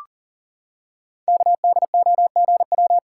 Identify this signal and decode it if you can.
Morse code, heard as a keyed tone, about 40 wpm, 700 Hz. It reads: KDOGW
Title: Code Groups - 5 Letters - 40wpm